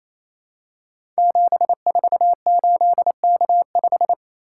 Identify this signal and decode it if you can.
Morse code, heard as a keyed tone, about 27 wpm, 700 Hz. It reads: 748K5
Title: Code Groups - 5 Letter:Numbers - 28wpm